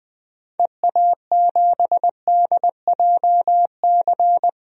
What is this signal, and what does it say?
Morse code, heard as a keyed tone, about 21 wpm, 700 Hz. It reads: EA7DJC
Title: DXCC-100 Call Signs - 6 Characters - 20wpm